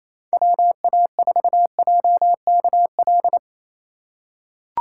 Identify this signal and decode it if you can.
Morse code, heard as a keyed tone, about 27 wpm, 700 Hz. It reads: WA4JKL
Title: K1USN SST QSO Practice - 28wpm